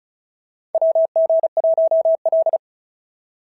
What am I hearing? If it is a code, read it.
Morse code, heard as a keyed tone, about 34 wpm, 650 Hz. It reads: WG1L
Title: SOTA QSO Practice 35wpm